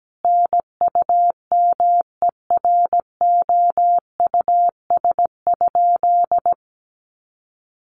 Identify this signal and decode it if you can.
Morse code, heard as a keyed tone, about 17 wpm, 700 Hz. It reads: NUMEROUS?
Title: Sentences from Top 1400 Words 17wpm